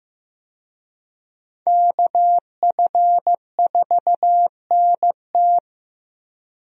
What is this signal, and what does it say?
Morse code, heard as a keyed tone, about 15 wpm, 700 Hz. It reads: KF4NT
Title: USA Call Signs - 5 Characters - Repeated 3x - 15wpm